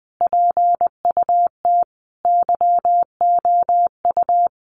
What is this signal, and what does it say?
Morse code, heard as a keyed tone, about 19 wpm, 700 Hz. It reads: PUT YOU
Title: Sentences from Top 1200 Words 20wpm